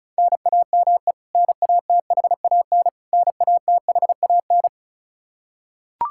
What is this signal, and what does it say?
Morse code, heard as a keyed tone, about 36 wpm, 700 Hz. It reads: NAME NATHAN NATHAN
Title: Practice QSO - Exchange 1 - Beginning - 35wpm